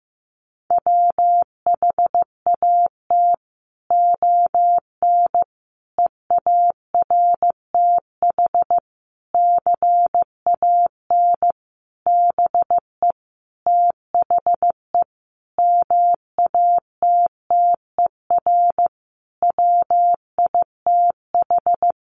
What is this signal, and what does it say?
Morse code, heard as a keyed tone, about 14 wpm, 700 Hz. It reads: WHAT ON EARTH CAN BE THE MATTER WITH
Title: Sentences from Top 700 Words 15wpm